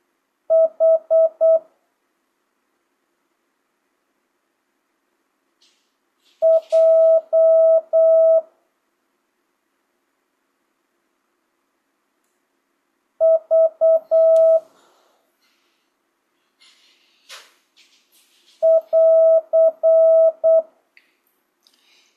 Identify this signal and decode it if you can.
Morse code, heard as a keyed tone, about 8 wpm, 650 Hz. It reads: H J V <AR>
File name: jh8cbh / 2025cw